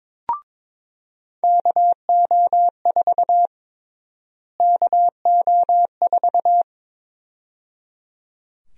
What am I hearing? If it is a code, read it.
Morse code, heard as a keyed tone, about 22 wpm, 700 Hz. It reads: KO4 KO4
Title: USA Call Sign Prefixes - Repeated 2x - 22wpm